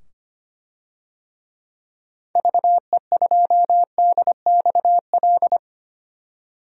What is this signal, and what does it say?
Morse code, heard as a keyed tone, about 25 wpm, 700 Hz. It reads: VE2DXL
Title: DXCC-100 Call Signs - 6 Characters - 25wpm